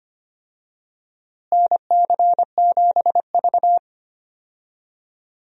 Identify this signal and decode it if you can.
Morse code, heard as a keyed tone, about 25 wpm, 700 Hz. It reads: NC7V
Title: USA Call Signs - 4 Characters - Repeated 2x - 25wpm